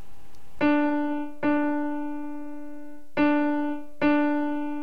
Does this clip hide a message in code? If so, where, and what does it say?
No.